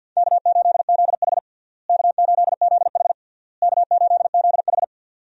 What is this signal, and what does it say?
Morse code, heard as a keyed tone, about 52 wpm, 700 Hz. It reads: K8ZH K8ZH K8ZH
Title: USA Call Signs - 4 Characters - Repeated 3x - 50wpm